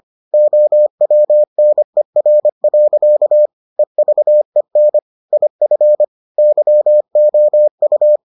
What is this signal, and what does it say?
Morse code, heard as a keyed tone, about 25 wpm, 600 Hz. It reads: OWNER. EVEN IF YOU